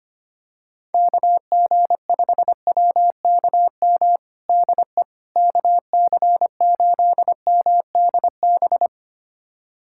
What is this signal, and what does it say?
Morse code, heard as a keyed tone, about 25 wpm, 700 Hz. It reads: KG5WKM DE KC8MDB
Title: USA Callsign DE Callsign 25wpm